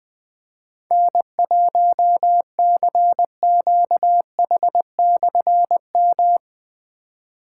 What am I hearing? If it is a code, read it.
Morse code, heard as a keyed tone, about 20 wpm, 700 Hz. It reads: N1CQH/M
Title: Difficult Call Signs 20wpm